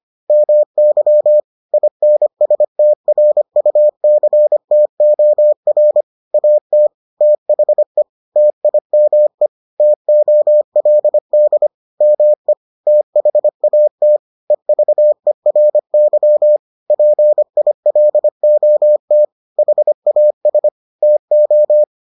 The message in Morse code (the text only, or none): MY INSTRUCTOR AT THE TIME TOLD ME THAT EVERY PILOT HAS TO